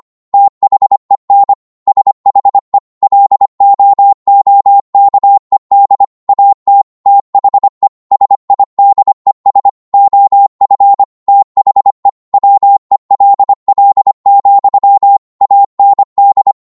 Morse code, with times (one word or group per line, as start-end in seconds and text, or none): 0.3-1.5 THEN
1.9-2.8 SHE
3.0-6.0 LOOKED
6.3-6.8 AT
7.1-7.9 THE
8.1-9.7 SIDES
9.9-11.0 OF
11.3-12.1 THE
12.3-15.2 WELL,
15.4-16.5 AND